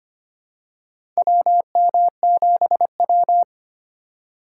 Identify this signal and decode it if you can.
Morse code, heard as a keyed tone, about 25 wpm, 700 Hz. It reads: WM7W
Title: USA Call Signs - 4 Characters - 3x Character Spacing - 25wpm